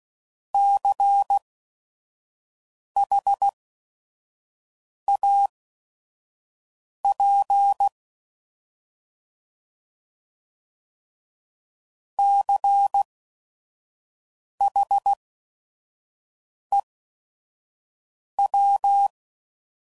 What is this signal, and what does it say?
Morse code, heard as a keyed tone, about 16 wpm, spaced as for 5 wpm, 800 Hz. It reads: CHAP CHEW
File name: Prefixes / By First Letter / Cs